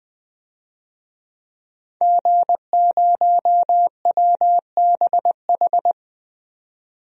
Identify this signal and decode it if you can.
Morse code, heard as a keyed tone, about 20 wpm, 700 Hz. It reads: G0WBH
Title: DXCC-100 Call Signs - 5 Characters - 20wpm